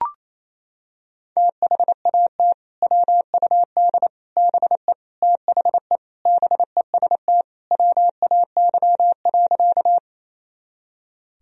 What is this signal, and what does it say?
Morse code, heard as a keyed tone, about 28 wpm, 700 Hz. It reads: THAT WUD BE THE BEST WAY.
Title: Sentences from Top 400 Words and Abbreviations 28wpm